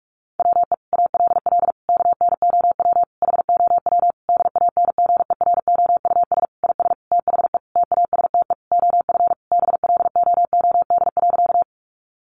Morse code, heard as a keyed tone, about 44 wpm, 700 Hz. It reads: WE ALL KNOW HOW DANGEROUS IS THE TASTE OF BLOOD.
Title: Sentences from Top 1200 Words 45wpm